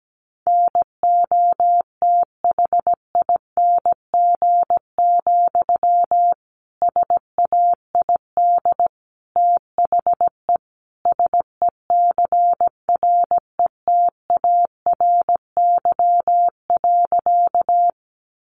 Morse code, heard as a keyed tone, about 17 wpm, 700 Hz. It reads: NOTHING, SAID THE SECRETARY.
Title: Sentences from Top 1300 Words 17wpm